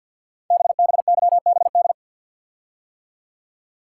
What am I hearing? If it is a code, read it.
Morse code, heard as a keyed tone, about 51 wpm, 700 Hz. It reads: BBYBD
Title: Code Groups - 5 Letters - 50wpm